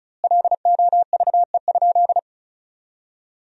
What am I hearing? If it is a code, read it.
Morse code, heard as a keyed tone, about 34 wpm, 700 Hz. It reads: LOVE?
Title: 2x Word-Spacing - Sentences from Top 500 Words - 35wpm